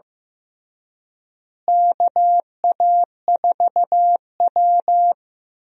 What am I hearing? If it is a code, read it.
Morse code, heard as a keyed tone, about 15 wpm, 700 Hz. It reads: KA4W
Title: USA Call Signs - 4 Characters - Repeated 3x - 15wpm